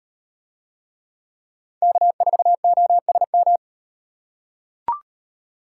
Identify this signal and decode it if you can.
Morse code, heard as a keyed tone, about 33 wpm, 700 Hz. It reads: K4OSM